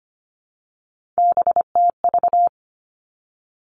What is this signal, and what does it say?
Morse code, heard as a keyed tone, about 25 wpm, 700 Hz. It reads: BTV
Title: Three Challenging Letter-Number Combinations 25wpm